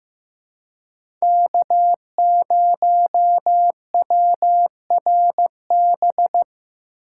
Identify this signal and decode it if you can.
Morse code, heard as a keyed tone, about 15 wpm, 700 Hz. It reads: K0WRB
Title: USA Call Signs - 2x Character Spacing - 15wpm